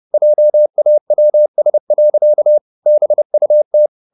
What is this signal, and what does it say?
Morse code, heard as a keyed tone, about 30 wpm, 600 Hz. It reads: JAWS. BUT